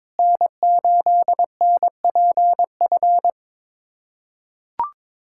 Morse code, heard as a keyed tone, about 22 wpm, 700 Hz. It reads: N8NPF
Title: USA Callsign DE Callsign 22wpm